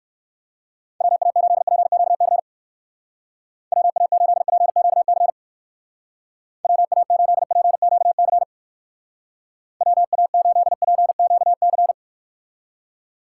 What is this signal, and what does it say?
Morse code, heard as a keyed tone, about 56 wpm, 700 Hz. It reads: WA8PQC WA8PQC WA8PQC WA8PQC
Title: Speed Racing - USA 2x3 Call Signs 45wpm